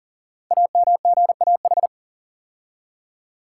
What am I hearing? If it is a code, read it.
Morse code, heard as a keyed tone, about 39 wpm, 690 Hz. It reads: AMGAH
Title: Code Groups - 5 Letters - 40wpm